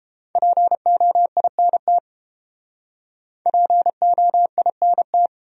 Morse code, heard as a keyed tone, about 30 wpm, 700 Hz. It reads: POINT POINT